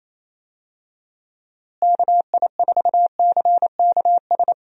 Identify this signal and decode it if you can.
Morse code, heard as a keyed tone, about 28 wpm, 700 Hz. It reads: KI4CKS
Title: USA Call Signs - 6 Characters - Repeated 3x - 28wpm